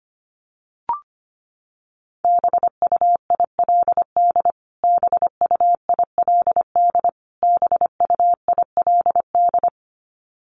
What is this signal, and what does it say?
Morse code, heard as a keyed tone, about 25 wpm, 700 Hz. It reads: BUILD BUILD BUILD
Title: Top 500 Words Repeated 25wpm